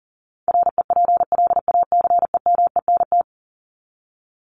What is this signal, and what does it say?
Morse code, heard as a keyed tone, about 40 wpm, 700 Hz. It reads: REPLACEMENT
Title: Top 1751-2000 Words 40wpm